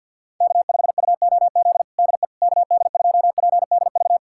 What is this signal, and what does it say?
Morse code, heard as a keyed tone, about 51 wpm, 700 Hz. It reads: K5UOZ DE KD2PDV
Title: USA Callsign DE Callsign 50wpm